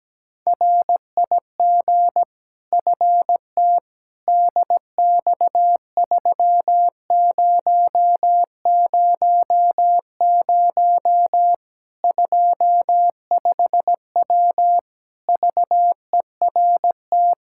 Morse code, heard as a keyed tone, about 17 wpm, 700 Hz. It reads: RIG FT DX3000 25W VERT
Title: Practice QSO - Exchange 2 - Info - 17wpm